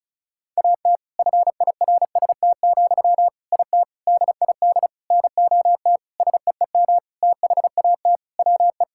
Morse code, heard as a keyed tone, about 36 wpm, 700 Hz. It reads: AT FIRST, IT DID NOT SEEM THAT WE
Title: Sentences from Top 1000 Words - Review Part 6 - 35wpm